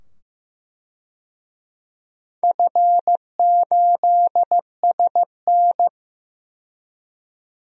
Morse code, heard as a keyed tone, about 15 wpm, 700 Hz. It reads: F8SN
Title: DXCC-100 Call Signs - 4 Characters - 15wpm